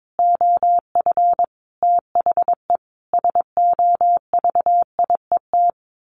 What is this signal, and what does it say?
Morse code, heard as a keyed tone, about 22 wpm, 700 Hz. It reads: OF THE SOVIET
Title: Sets of 5 Words Encore 22wpm